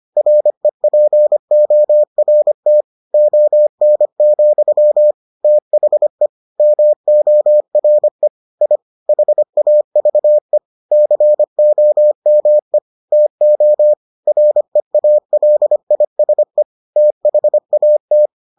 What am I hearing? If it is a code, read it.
Morse code, heard as a keyed tone, about 25 wpm, 600 Hz. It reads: REPORT ON, THE MORE I HAVE COME TO REALISE THAT